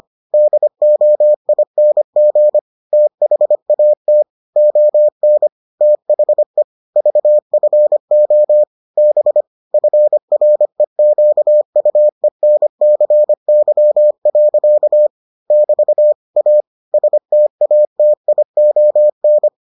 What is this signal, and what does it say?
Morse code, heard as a keyed tone, about 25 wpm, 600 Hz. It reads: DOING THAT ON THE VFO B FREQUENCY. <BT> A STATION